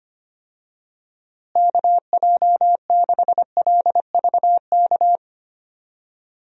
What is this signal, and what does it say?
Morse code, heard as a keyed tone, about 25 wpm, 700 Hz. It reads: KJ6LVK
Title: USA Call Signs - 6 Characters - Repeated 3x - 25wpm